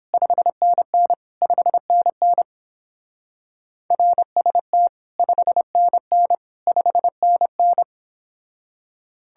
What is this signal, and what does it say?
Morse code, heard as a keyed tone, about 29 wpm, 700 Hz. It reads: 5NN 5NN RST 5NN 5NN